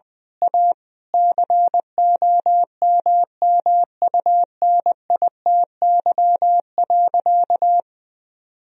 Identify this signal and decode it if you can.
Morse code, heard as a keyed tone, about 20 wpm, 700 Hz. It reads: A COMMUNITY.
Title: Sentences from Top 1000 Words - Review Part 4 - 20wpm